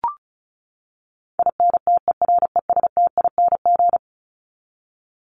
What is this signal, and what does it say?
Morse code, heard as a keyed tone, about 34 wpm, 700 Hz. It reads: INTERESTING
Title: ING Suffix - Top 100 Words with Base Words 35wpm